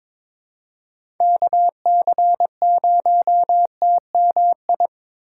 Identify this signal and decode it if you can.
Morse code, heard as a keyed tone, about 22 wpm, 700 Hz. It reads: KC0TMI
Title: USA Call Signs - 6 Characters - 4x Character Spacing - 22wpm